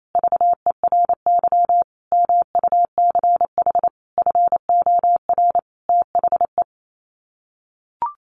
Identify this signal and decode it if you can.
Morse code, heard as a keyed tone, about 28 wpm, 700 Hz. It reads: VERY MUCH FOR THE
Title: Sets of 5 Words Encore2 28wpm